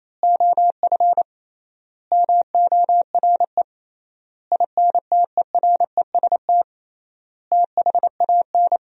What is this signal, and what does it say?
Morse code, heard as a keyed tone, about 28 wpm, 700 Hz. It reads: OF MORE INTEREST THAN
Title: 3x Word-Spacing - Sentences from Top 400 Words - 28wpm